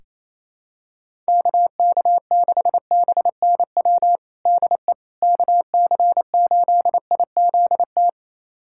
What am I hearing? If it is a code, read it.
Morse code, heard as a keyed tone, about 28 wpm, 700 Hz. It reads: KK6BNW DE KC8IZT
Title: USA Callsign DE Callsign 28wpm